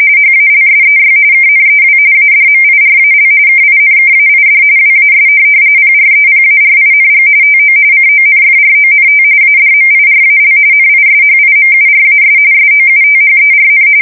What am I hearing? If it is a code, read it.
RTTY, about 45.45 baud. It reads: HALSEY KNEW ADMIRAL KINKAID WAS PREPARING TO CONFRONT ADMIRAL NISHIMURAS SOUTHERN
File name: fsk170 / USS-New-Jersey